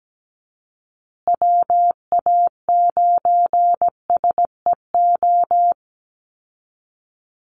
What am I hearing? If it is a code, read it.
Morse code, heard as a keyed tone, about 17 wpm, 700 Hz. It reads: WA9SEO
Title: Call Signs Repeated 3x - 17wpm